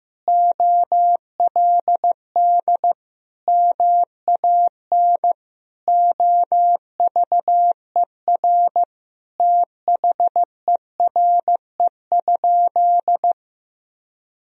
Morse code, heard as a keyed tone, about 15 wpm, 700 Hz. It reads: OLD MAN OVER THERE?
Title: Sentences from Top 200 Words and Abbreviations 15wpm